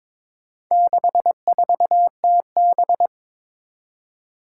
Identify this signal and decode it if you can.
Morse code, heard as a keyed tone, about 23 wpm, 700 Hz. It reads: 64TB
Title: Four Challenging Letter-Number Combinations 22wpm